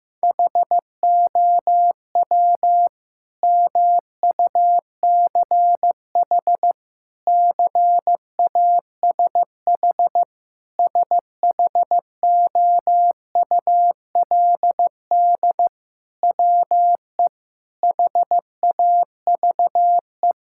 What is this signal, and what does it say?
Morse code, heard as a keyed tone, about 14 wpm, 700 Hz. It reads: HOW MUCH CASH SHOULD WE HAVE
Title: Sentences from Top 1000 Words - Review Part 5 - 15wpm